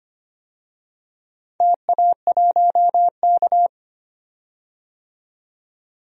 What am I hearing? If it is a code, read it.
Morse code, heard as a keyed tone, about 25 wpm, 700 Hz. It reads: TA1K
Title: DXCC-100 Call Signs - 4 Characters - 25wpm